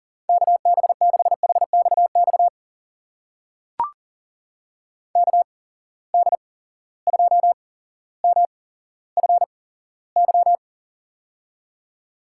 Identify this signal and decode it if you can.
Morse code, heard as a keyed tone, about 43 wpm, spaced as for 17 wpm, 700 Hz. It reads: KB6HXX KD2MFY
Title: USA Call Signs - 6 Characters - 8x Character Spacing - 40wpm